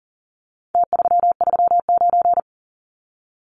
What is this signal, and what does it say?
Morse code, heard as a keyed tone, about 40 wpm, 700 Hz. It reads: T339
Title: Numbers - Contest Style - 4-Digit 40wpm